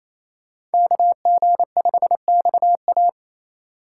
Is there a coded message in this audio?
Yes: Morse code from 0.7 to 3.1 s, heard as a keyed tone, about 36 wpm, 700 Hz. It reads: KG5XA